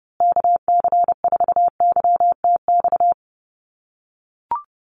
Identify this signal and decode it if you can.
Morse code, heard as a keyed tone, about 30 wpm, 700 Hz. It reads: KC4YTX
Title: USA Call Signs Repeated 3x 30wpm